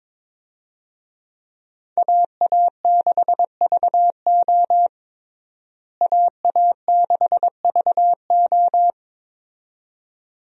Morse code, heard as a keyed tone, about 22 wpm, 700 Hz. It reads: AA6VO AA6VO
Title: USA Call Signs - 5 Characters - Repeated 2x - 22wpm